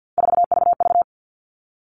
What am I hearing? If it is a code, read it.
Morse code, heard as a keyed tone, about 50 wpm, 700 Hz. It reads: 4VV
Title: All Cut Numbers - 3-Digit 50wpm